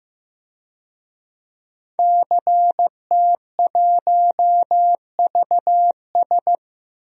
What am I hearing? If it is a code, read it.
Morse code, heard as a keyed tone, about 16 wpm, 700 Hz. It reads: CT1VS
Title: DXCC-100 Call Signs - 5 Characters - Repeated 2x - 15wpm